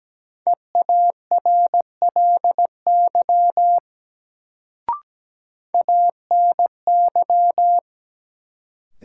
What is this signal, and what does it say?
Morse code, heard as a keyed tone, about 17 wpm, 700 Hz. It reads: EARLY ANY
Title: Sets of 3 Words - New Word Drill - 17wpm